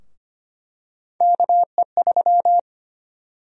Morse code, heard as a keyed tone, about 25 wpm, 700 Hz. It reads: KE3
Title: USA Call Sign Prefixes - Repeated 2x - 25wpm